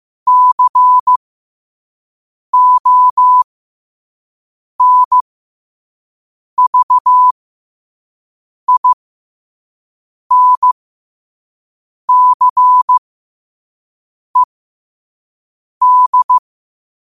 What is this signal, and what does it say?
Morse code, heard as a keyed tone, about 15 wpm, spaced as for 5 wpm, 1000 Hz. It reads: CONVINCED
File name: qotdfiles / 5wpm